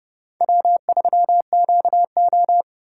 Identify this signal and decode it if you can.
Morse code, heard as a keyed tone, about 29 wpm, 700 Hz. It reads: W3QO